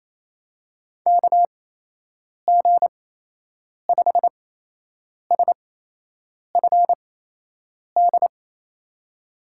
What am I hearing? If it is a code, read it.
Morse code, heard as a keyed tone, about 27 wpm, spaced as for 8 wpm, 700 Hz. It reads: KG5SFD
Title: USA Call Signs - 6 Characters - 8x Character Spacing - 28wpm